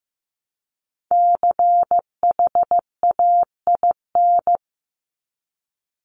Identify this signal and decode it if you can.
Morse code, heard as a keyed tone, about 15 wpm, 700 Hz. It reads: CHAIN
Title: Top 1251-1500 Words 15wpm